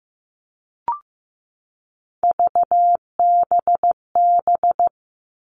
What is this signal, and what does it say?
Morse code, heard as a keyed tone, about 15 wpm, 700 Hz. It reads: VBB
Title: Three Challenging Letter-Number Combinations 15wpm